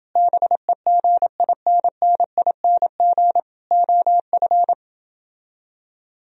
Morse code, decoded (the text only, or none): BEGINNING OF